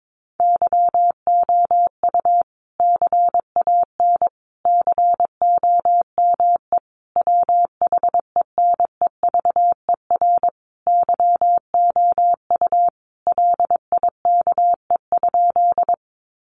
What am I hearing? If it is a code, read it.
Morse code, heard as a keyed tone, about 22 wpm, 700 Hz. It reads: YOU CAN COME WHENEVER YOU LIKE?
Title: Sentences from Top 1200 Words 22wpm